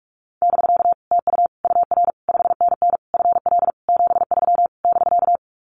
Morse code, heard as a keyed tone, about 44 wpm, 700 Hz. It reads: <BK> TU UR 5NN FL 73 <BK>
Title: Common QSO Phrases Encore 45wpm